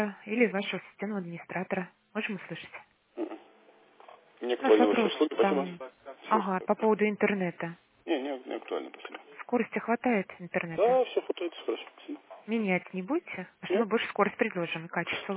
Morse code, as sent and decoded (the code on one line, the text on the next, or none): none